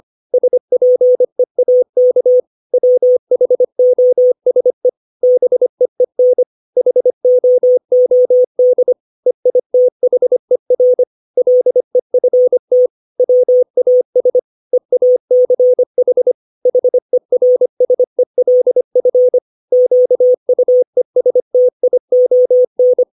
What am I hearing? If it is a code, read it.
Morse code, heard as a keyed tone, about 25 wpm, 500 Hz. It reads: SPEAK WHOSE BEEN HOOD EITHER LEFT WAS EACH HERSELF QUESTION
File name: 632 words / 632 Words 25WPM